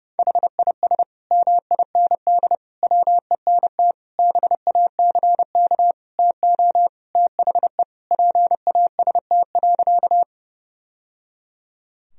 Morse code, heard as a keyed tone, about 30 wpm, 700 Hz. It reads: HIS MIND WENT BACK TO THE PAST.
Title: Sentences from Top 1000 Words - Review Part 3 - 30wpm